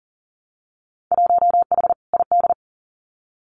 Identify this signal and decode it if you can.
Morse code, heard as a keyed tone, about 41 wpm, 700 Hz. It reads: 1H ID
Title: Winter Field Day Exchanges 40wpm